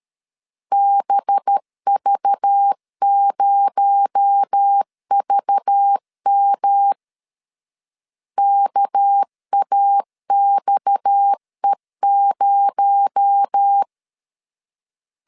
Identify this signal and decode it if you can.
Morse code, heard as a keyed tone, about 13 wpm, 800 Hz. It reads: BV0VM KAXE0
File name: LEZIONI / 6 PKX0 / 6 PKX0 12wpm 2231